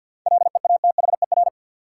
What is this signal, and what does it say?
Morse code, heard as a keyed tone, about 48 wpm, 700 Hz. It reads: LEATHER
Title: Seven Letter Words - 50wpm